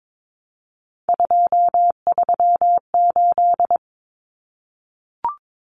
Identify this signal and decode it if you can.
Morse code, heard as a keyed tone, about 22 wpm, 700 Hz. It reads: 238
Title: Numbers - 3-Digit 22wpm